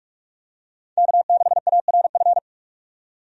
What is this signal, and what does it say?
Morse code, heard as a keyed tone, about 46 wpm, 700 Hz. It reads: K6ARF
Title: USA Call Signs - 5 Characters - 45wpm